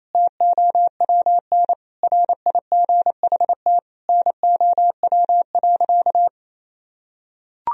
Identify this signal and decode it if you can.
Morse code, heard as a keyed tone, about 28 wpm, 700 Hz. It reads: TOWN RIGHT NOW.
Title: Sentences from Top 1000 Words - Review Part 4 - 28wpm